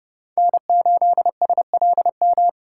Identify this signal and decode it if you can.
Morse code, heard as a keyed tone, about 29 wpm, 700 Hz. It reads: N8SLM
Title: USA Call Signs - 3x Character Spacing - 30wpm